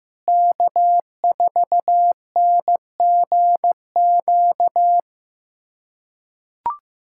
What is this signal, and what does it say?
Morse code, heard as a keyed tone, about 15 wpm, 700 Hz. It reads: K4NGQ
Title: USA Callsign DE Callsign - 5 Characters - 15wpm